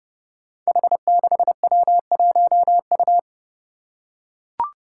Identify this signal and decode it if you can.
Morse code, heard as a keyed tone, about 30 wpm, 700 Hz. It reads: H6W1U
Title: Five Challenging Letter-Number Combinations 30wpm